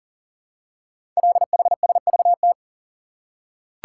Morse code, heard as a keyed tone, about 40 wpm, 700 Hz. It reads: LHSVT
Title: Code Groups - 5 Letters - 40wpm